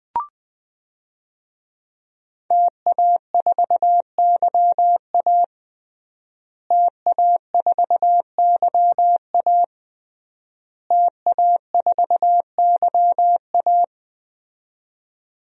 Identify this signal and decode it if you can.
Morse code, heard as a keyed tone, about 21 wpm, 700 Hz. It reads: TA4YA TA4YA TA4YA
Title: DXCC-100 Call Signs - 5 Characters - Repeated 3x - 20wpm